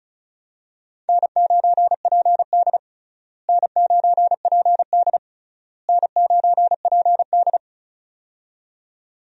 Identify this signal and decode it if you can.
Morse code, heard as a keyed tone, about 36 wpm, 700 Hz. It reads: N9PD N9PD N9PD
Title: USA Call Signs - 4 Characters - Repeated 3x - 35wpm